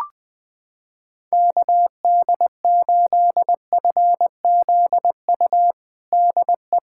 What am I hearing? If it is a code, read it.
Morse code, heard as a keyed tone, about 20 wpm, 700 Hz. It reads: KD8FZU DE
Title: USA Callsign DE Callsign 20wpm